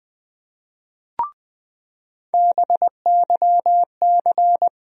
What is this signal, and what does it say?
Morse code, heard as a keyed tone, about 20 wpm, 700 Hz. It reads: BYC
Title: USA Call Sign Postfixes - 3 Letters 20wpm